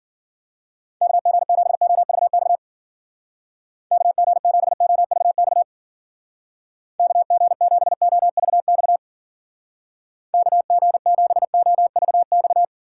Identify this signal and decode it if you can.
Morse code, heard as a keyed tone, about 49 wpm, 700 Hz. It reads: KG7OVX KG7OVX KG7OVX KG7OVX
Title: Speed Racing - USA 2x3 Call Signs 40wpm